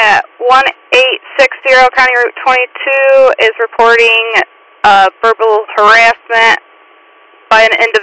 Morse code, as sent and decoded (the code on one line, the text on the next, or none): none